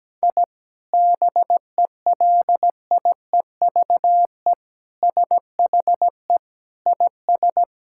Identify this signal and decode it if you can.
Morse code, heard as a keyed tone, about 17 wpm, 700 Hz. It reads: I BELIEVE SHE IS
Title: Sentences from Top 1000 Words - Review Part 5 - 17wpm